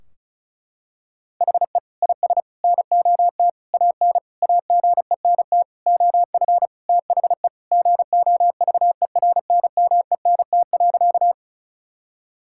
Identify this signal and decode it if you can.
Morse code, heard as a keyed tone, about 36 wpm, 700 Hz. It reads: HE IS NOT AN AGENT OF THE GOVERNMENT.
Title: Sentences from Top 1600 Words 35wpm